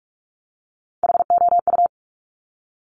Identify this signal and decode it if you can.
Morse code, heard as a keyed tone, about 46 wpm, 700 Hz. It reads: HOU
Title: Top Three Letter Combinations in Words 45wpm